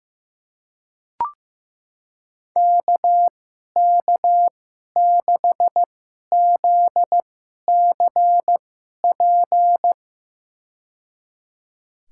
Morse code, heard as a keyed tone, about 15 wpm, spaced as for 11 wpm, 700 Hz. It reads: KK6ZCP
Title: USA Call Signs - 6 Characters - 2x Character Spacing - 15wpm